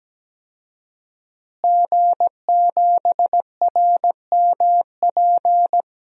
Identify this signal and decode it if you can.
Morse code, heard as a keyed tone, about 17 wpm, 700 Hz. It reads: G7RMP
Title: DXCC-100 Call Signs - 5 Characters - Repeated 3x - 17wpm